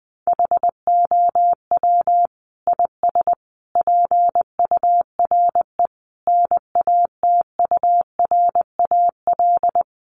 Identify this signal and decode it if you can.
Morse code, heard as a keyed tone, about 20 wpm, 700 Hz. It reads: HOW IS PURE NATURAL
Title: Sentences from Top 1500 Words 20wpm